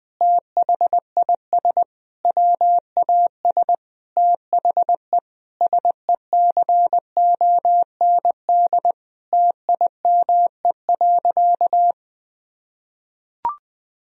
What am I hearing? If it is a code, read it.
Morse code, heard as a keyed tone, about 20 wpm, 700 Hz. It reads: THIS WAS THE SECOND TIME.
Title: Sentences from Top 1000 Words - Review Part 1 - 20wpm